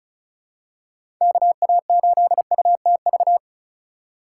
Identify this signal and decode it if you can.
Morse code, heard as a keyed tone, about 36 wpm, 700 Hz. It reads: KA8UTV
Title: USA Call Signs - 6 Characters - 2x Character Spacing - 35wpm